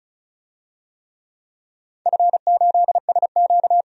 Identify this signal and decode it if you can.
Morse code, heard as a keyed tone, about 34 wpm, 700 Hz. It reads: F8SQ
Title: DXCC-100 Call Signs - 4 Characters - 35wpm